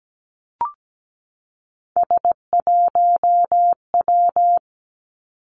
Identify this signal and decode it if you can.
Morse code, heard as a keyed tone, about 17 wpm, 700 Hz. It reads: S1W
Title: Three Challenging Letter-Number Combinations 17wpm